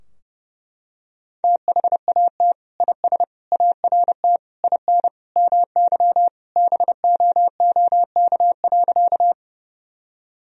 Morse code, heard as a keyed tone, about 32 wpm, 700 Hz. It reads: THAT IS ART IN MY BOOK.
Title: Sentences from Top 1000 Words - Review Part 3 - 30wpm